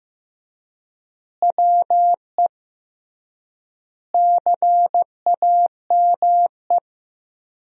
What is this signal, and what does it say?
Morse code, heard as a keyed tone, about 15 wpm, 700 Hz. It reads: WE CAME
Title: 3x Word-Spacing - Sentences from Top 400 Words - 15wpm